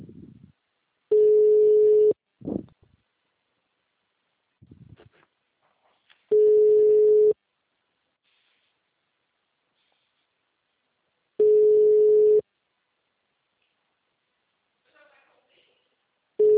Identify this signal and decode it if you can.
Ringback tone (Europe, 425 Hz), with speech.
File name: out- -2021- - - .5399